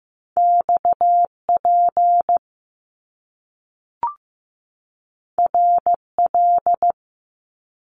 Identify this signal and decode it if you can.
Morse code, heard as a keyed tone, about 15 wpm, 700 Hz. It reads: XP RL
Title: Top Two Letter Combinations in Words 15wpm